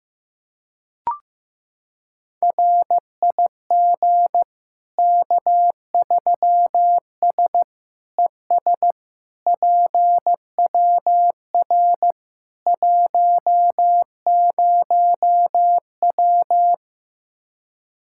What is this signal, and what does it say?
Morse code, heard as a keyed tone, about 15 wpm, 700 Hz. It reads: RIG K3S ES PWR 10W
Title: Practice QSO - Exchange 2 - Info - 15wpm